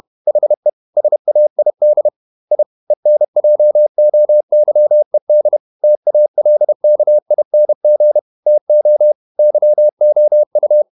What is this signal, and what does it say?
Morse code, heard as a keyed tone, about 31 wpm, 600 Hz. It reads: HE SAID I ENJOYED TALKING TO YOU
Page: Never 31wpm | CW med Gnister